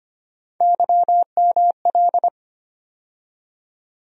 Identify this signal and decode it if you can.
Morse code, heard as a keyed tone, about 25 wpm, 700 Hz. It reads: YML
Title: USA Call Sign Postfixes - 3 Letters 25wpm